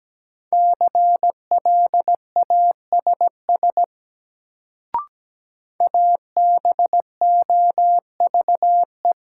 Morse code, heard as a keyed tone, about 17 wpm, 700 Hz. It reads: CLASS ABOVE
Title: Sentences from Top 1000 Words - Review Part 2 - New Word Drill - 17wpm